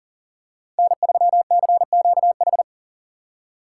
Morse code, heard as a keyed tone, about 37 wpm, 700 Hz. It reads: N3CQH
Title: USA Call Signs - 5 Characters - 4x Character Spacing - 40wpm